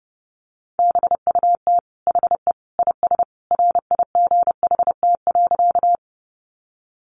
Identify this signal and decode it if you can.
Morse code, heard as a keyed tone, about 30 wpm, 700 Hz. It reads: BUT HE IS RIGHT.
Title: Sentences from Top 200 Words 30wpm